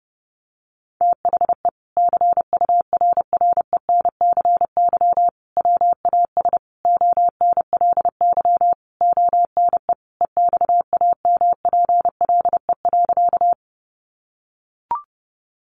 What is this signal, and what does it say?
Morse code, heard as a keyed tone, about 28 wpm, 700 Hz. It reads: THE CURRENCY WAS ONLY ONE EXAMPLE.
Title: Sentences from Top 1500 Words 30wpm